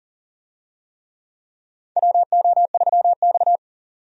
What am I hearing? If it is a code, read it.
Morse code, heard as a keyed tone, about 40 wpm, 700 Hz. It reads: WO3X
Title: USA Call Signs - 4 Characters - 40wpm